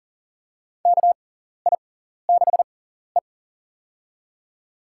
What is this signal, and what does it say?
Morse code, heard as a keyed tone, about 40 wpm, spaced as for 14 wpm, 700 Hz. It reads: KI6E
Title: USA Call Signs - 4 Characters - 6x Character Spacing - 40wpm